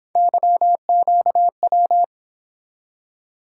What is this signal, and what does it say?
Morse code, heard as a keyed tone, about 28 wpm, 700 Hz. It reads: YQW